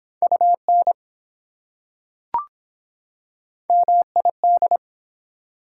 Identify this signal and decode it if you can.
Morse code, heard as a keyed tone, about 22 wpm, 700 Hz. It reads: UN MID